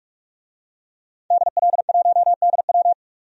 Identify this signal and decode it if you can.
Morse code, heard as a keyed tone, about 42 wpm, 700 Hz. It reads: DL1DW
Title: DXCC-100 Call Signs - 5 Characters - 45wpm